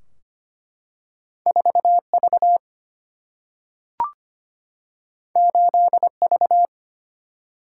Morse code, heard as a keyed tone, about 25 wpm, 700 Hz. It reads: 4V 8V
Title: Single-Letter Single-Number 25wpm